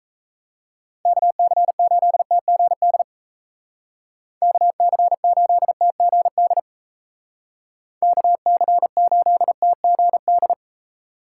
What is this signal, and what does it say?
Morse code, heard as a keyed tone, about 36 wpm, 700 Hz. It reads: KC8TGD KC8TGD KC8TGD